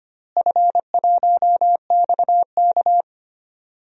Morse code, heard as a keyed tone, about 25 wpm, 700 Hz. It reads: F1XK
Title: DXCC-100 Call Signs - 4 Characters - Repeated 2x - 25wpm